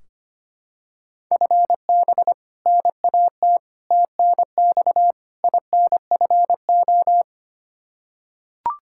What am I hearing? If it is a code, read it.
Morse code, heard as a keyed tone, about 25 wpm, 700 Hz. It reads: FB NAT TNX INFO
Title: Practice QSO - Exchange 3 - More Info - 25wpm